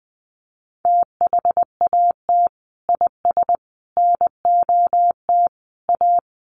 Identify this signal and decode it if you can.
Morse code, heard as a keyed tone, about 20 wpm, 700 Hz. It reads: THAT IS NOT A
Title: Sentences from Top 1400 Words 20wpm